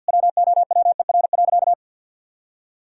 Morse code, heard as a keyed tone, about 51 wpm, 700 Hz. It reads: WOWER.
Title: Pangram Challenge v2 - 50wpm